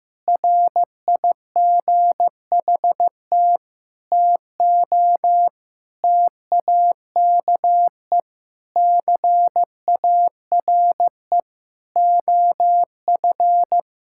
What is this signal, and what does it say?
Morse code, heard as a keyed tone, about 16 wpm, 700 Hz. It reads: RIGHT TO TAKE CARE OF
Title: Sentences from Top 1000 Words - Review Part 5 - 15wpm